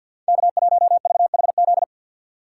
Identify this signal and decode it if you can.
Morse code, heard as a keyed tone, about 52 wpm, 700 Hz. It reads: K1VHZ
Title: USA Call Signs - 5 Characters - 6x Character Spacing - 50wpm